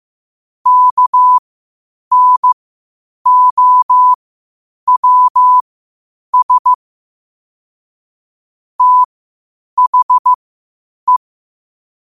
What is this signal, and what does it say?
Morse code, heard as a keyed tone, about 15 wpm, spaced as for 8 wpm, 1000 Hz. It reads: KNOWS THE
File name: qotdfiles / 8wpm